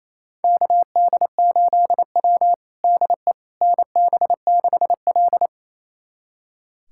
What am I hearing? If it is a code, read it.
Morse code, heard as a keyed tone, about 28 wpm, 700 Hz. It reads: KD8W DE NB6L
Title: USA Callsign DE Callsign - 4 Characters - 28wpm